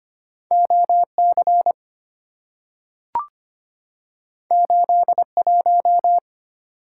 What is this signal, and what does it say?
Morse code, heard as a keyed tone, about 25 wpm, 690 Hz. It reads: OC 81
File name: Two Letter-Number Combinations - 25wpm